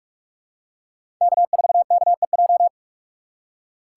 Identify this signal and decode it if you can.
Morse code, heard as a keyed tone, about 42 wpm, 700 Hz. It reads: K4KEJ
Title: USA Call Signs - 5 Characters - 3x Character Spacing - 45wpm